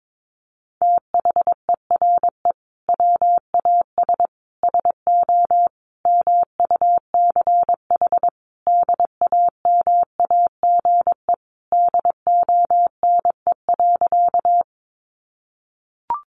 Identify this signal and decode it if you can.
Morse code, heard as a keyed tone, about 22 wpm, 700 Hz. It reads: THERE WAS SO MUCH DAMAGE DONE.
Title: Sentences from Top 1100 Words 22wpm